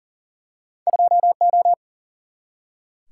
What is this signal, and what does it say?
Morse code, heard as a keyed tone, about 41 wpm, 700 Hz. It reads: 2 O
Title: Single-Letter Single-Number 40wpm